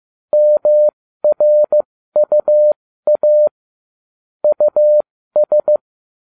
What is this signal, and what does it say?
Morse code, heard as a keyed tone, about 15 wpm, spaced as for 13 wpm, 600 Hz. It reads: MRUA US